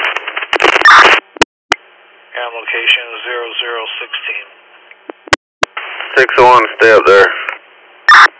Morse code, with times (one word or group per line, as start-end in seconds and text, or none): none